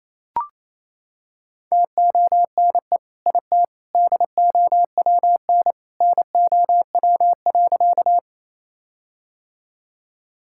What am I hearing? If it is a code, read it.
Morse code, heard as a keyed tone, about 28 wpm, 700 Hz. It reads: TONE IT DOWN NOW.
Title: Sentences from Top 1600 Words 28wpm